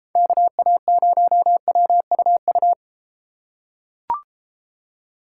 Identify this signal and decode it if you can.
Morse code, heard as a keyed tone, about 33 wpm, 700 Hz. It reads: KA0WUU